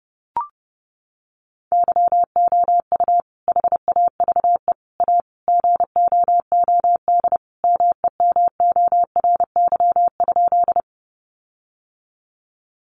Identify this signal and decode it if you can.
Morse code, heard as a keyed tone, about 30 wpm, 700 Hz. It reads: YOU HAVE A GOOD MEMORY?
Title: Sentences from Top 1300 Words 30wpm